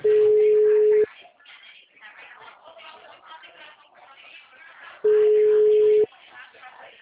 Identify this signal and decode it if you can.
Ringback tone (Europe, 425 Hz), with speech.
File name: out- -2022- - - .30070